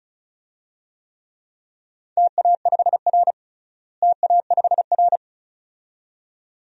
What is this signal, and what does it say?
Morse code, heard as a keyed tone, about 34 wpm, 700 Hz. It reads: TA5R TA5R
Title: DXCC-100 Call Signs - 4 Characters - Repeated 2x - 35wpm